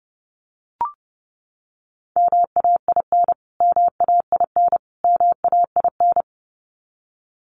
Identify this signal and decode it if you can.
Morse code, heard as a keyed tone, about 30 wpm, 700 Hz. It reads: MAIN MAIN MAIN
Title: Top 300 Words Repeated 30wpm